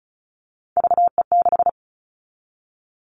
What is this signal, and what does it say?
Morse code, heard as a keyed tone, about 36 wpm, 700 Hz. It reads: VE6
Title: Callsign Prefixes for Top 100 DXCC - 35wpm